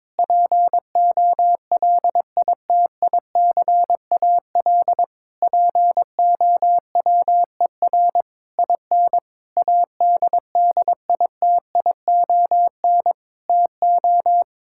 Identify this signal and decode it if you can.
Morse code, heard as a keyed tone, about 22 wpm, 700 Hz. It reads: POLITICAL POWER IN ADDITION TO
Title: Sentences from Top 1000 Words - Review Part 6 - 22wpm